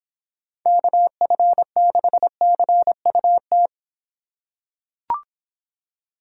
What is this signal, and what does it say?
Morse code, heard as a keyed tone, about 26 wpm, 700 Hz. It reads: KF6CUT